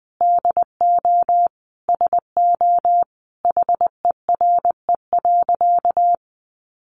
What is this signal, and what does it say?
Morse code, heard as a keyed tone, about 20 wpm, 700 Hz. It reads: DO SO HERE.
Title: Sentences from Top 600 Words 20wpm